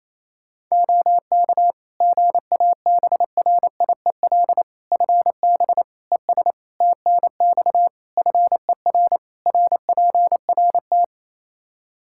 Morse code, heard as a keyed tone, about 27 wpm, 700 Hz. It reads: OK GABRIEL FB ES TNX FER RPRT
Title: Practice QSO - Exchange 2 - Info - 28wpm